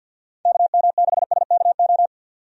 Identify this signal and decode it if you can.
Morse code, heard as a keyed tone, about 52 wpm, 700 Hz. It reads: KM6SKO
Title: USA Call Signs - 6 Characters - 50wpm